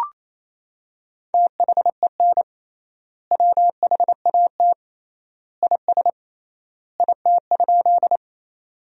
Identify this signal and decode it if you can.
Morse code, heard as a keyed tone, about 28 wpm, 700 Hz. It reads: THEN WHAT IS IT?
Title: 3x Word-Spacing - Sentences from Top 100 Words - 28wpm